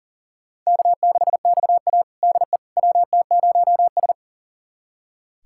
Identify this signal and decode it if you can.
Morse code, heard as a keyed tone, about 40 wpm, 700 Hz. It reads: K6XA DE WT0S
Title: USA Callsign DE Callsign - 4 Characters - 40wpm